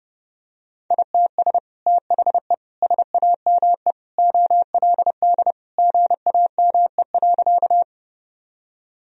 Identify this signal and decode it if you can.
Morse code, heard as a keyed tone, about 32 wpm, 700 Hz. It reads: ITS THE SAME OLD GAME.
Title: Sentences from Top 1000 Words - Review Part 3 - 30wpm